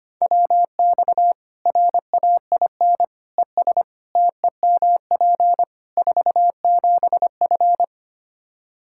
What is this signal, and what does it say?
Morse code, heard as a keyed tone, about 25 wpm, 700 Hz. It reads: WX RAIN ES TEMP 47F
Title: Practice QSO - Exchange 2 - Info - 25wpm